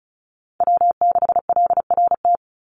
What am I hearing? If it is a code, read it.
Morse code, heard as a keyed tone, about 34 wpm, 700 Hz. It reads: W6LRT
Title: USA Call Signs 35wpm